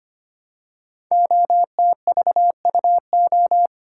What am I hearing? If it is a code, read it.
Morse code, heard as a keyed tone, about 25 wpm, 700 Hz. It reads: OTVUO
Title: Code Groups - 5 Letter:Numbers - 25wpm